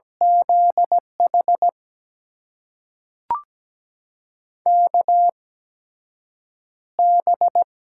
Morse code, heard as a keyed tone, about 17 wpm, spaced as for 7 wpm, 700 Hz. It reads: ZH KB
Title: USA Call Signs - 6 Characters - 8x Character Spacing - 17wpm